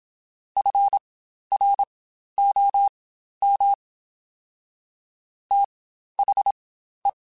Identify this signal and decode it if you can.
Morse code, heard as a keyed tone, about 27 wpm, spaced as for 12 wpm, 800 Hz. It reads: FROM THE
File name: Spaced Characters / Spaced Characters-mp3s / Weather5 5 sp @25